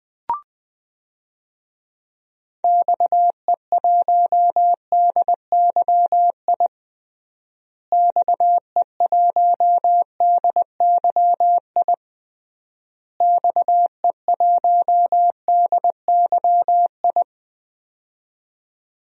Morse code, heard as a keyed tone, about 20 wpm, 700 Hz. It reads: XE1DYI XE1DYI XE1DYI
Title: DXCC-100 Call Signs - 6 Characters - Repeated 3x - 20wpm